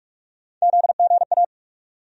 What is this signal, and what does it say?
Morse code, heard as a keyed tone, about 42 wpm, 700 Hz. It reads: ZGA
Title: USA Call Sign Postfixes - 3 Letters 45wpm